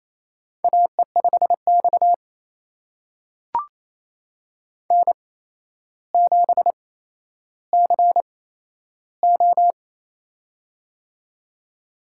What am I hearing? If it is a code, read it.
Morse code, heard as a keyed tone, about 28 wpm, spaced as for 12 wpm, 700 Hz. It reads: AE5X N7CO
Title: USA Call Signs - 4 Characters - 8x Character Spacing - 28wpm